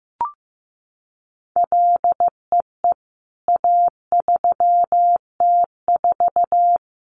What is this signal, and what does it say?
Morse code, heard as a keyed tone, about 15 wpm, 700 Hz. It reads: LEE A3T4
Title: CWT Exchange 15wpm